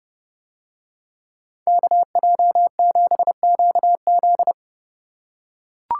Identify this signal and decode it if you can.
Morse code, heard as a keyed tone, about 27 wpm, 700 Hz. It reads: KJ7QZ